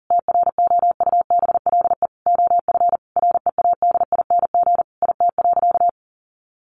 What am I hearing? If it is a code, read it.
Morse code, heard as a keyed tone, about 40 wpm, 700 Hz. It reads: TROUBLE OF READING IT.
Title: Sentences from Top 1200 Words 40wpm